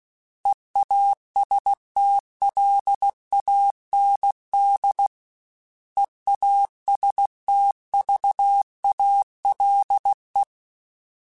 Morse code, heard as a keyed tone, about 16 wpm, 800 Hz. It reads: EASTLAND EASTVALE
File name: fromSMs / FromSMs mp3s / Cities@15